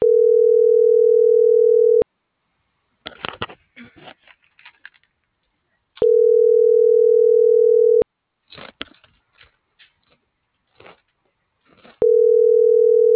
Ringback tone (North America), 440 + 480 Hz.